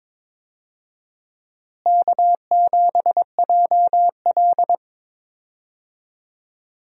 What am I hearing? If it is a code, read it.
Morse code, heard as a keyed tone, about 22 wpm, 700 Hz. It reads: K7JL
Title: USA Call Signs - 4 Characters - Repeated 2x - 22wpm